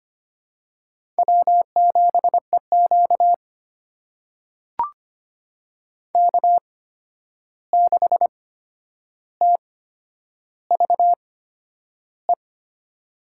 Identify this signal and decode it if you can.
Morse code, heard as a keyed tone, about 25 wpm, spaced as for 10 wpm, 700 Hz. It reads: W7EQ K6TVE
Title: USA Call Signs - 8x Character Spacing - 25wpm